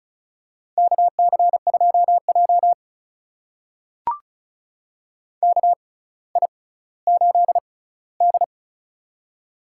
Morse code, heard as a keyed tone, about 36 wpm, spaced as for 18 wpm, 700 Hz. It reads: KC2J KI8D
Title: USA Call Signs - 4 Characters - 6x Character Spacing - 35wpm